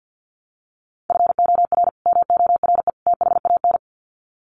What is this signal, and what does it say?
Morse code, heard as a keyed tone, about 50 wpm, 700 Hz. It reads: FOR MORE THAN
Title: Sets of 3 Words 50wpm